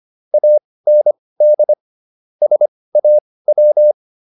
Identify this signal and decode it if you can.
Morse code, heard as a keyed tone, about 25 wpm, spaced as for 18 wpm, 600 Hz. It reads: AND SAW